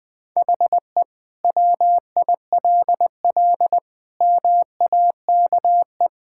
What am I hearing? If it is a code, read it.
Morse code, heard as a keyed tone, about 20 wpm, 700 Hz. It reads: HE WILL MAKE
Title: Sentences from Top 1000 Words - Review Part 5 - 20wpm